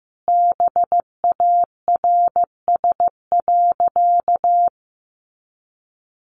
Morse code, heard as a keyed tone, about 15 wpm, 700 Hz. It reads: BARS.
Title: Easy Phrases 15wpm